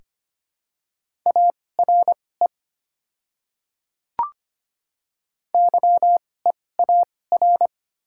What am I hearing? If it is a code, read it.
Morse code, heard as a keyed tone, about 25 wpm, spaced as for 18 wpm, 700 Hz. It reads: ARE YEAR
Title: 2x Character Spacing - Top 200 Words - 25wpm